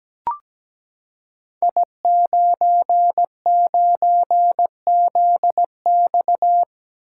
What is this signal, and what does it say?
Morse code, heard as a keyed tone, about 17 wpm, 700 Hz. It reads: I99ZX
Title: Code Groups - 5 Letter:Numbers - 17wpm